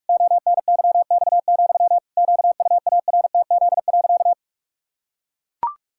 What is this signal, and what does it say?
Morse code, heard as a keyed tone, about 41 wpm, 700 Hz. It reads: ONYX, QUARTZ.
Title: Pangram Challenge v2 - 45wpm